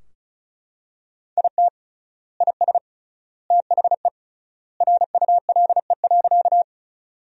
Morse code, heard as a keyed tone, about 34 wpm, 700 Hz. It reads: IT IS THE RULE.
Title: 3x Word-Spacing - Sentences from Top 400 Words - 35wpm